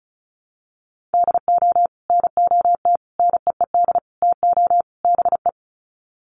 Morse code, heard as a keyed tone, about 34 wpm, 700 Hz. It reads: DO NOT NEED TO BE
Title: Sets of 5 Words Encore 35wpm